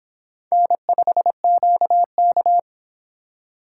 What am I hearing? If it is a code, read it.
Morse code, heard as a keyed tone, about 25 wpm, 700 Hz. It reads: N5QK